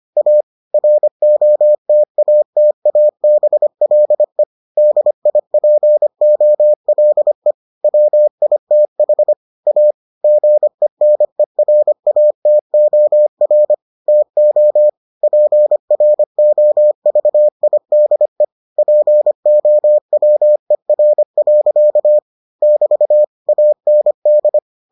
Morse code, heard as a keyed tone, about 25 wpm, 600 Hz. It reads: A ROTATABLE DIPOLE WITH A GENERATOR TO PROVIDE POWER. <BT> AND